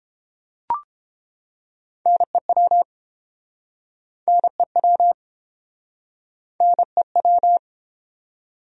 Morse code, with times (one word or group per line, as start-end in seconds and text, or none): 2.1-2.8 NEW
4.3-5.1 NEW
6.6-7.6 NEW